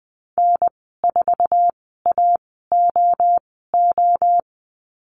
Morse code, heard as a keyed tone, about 20 wpm, spaced as for 15 wpm, 700 Hz. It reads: N4AOO
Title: 062 Lesson 11g - Letter h - Call Signs 20wpm